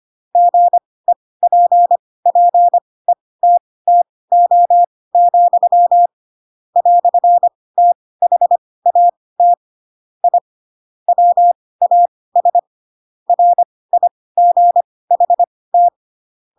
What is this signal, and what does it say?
Morse code, heard as a keyed tone, about 25 wpm, spaced as for 18 wpm, 700 Hz. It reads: GEPPETTO, "THAT I WAS RIGHT
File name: Pinocchio / Pinocchio ch7 12 25 ik1hge